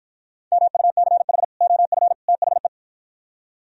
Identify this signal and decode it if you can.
Morse code, heard as a keyed tone, about 53 wpm, 690 Hz. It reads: MUCH OF THE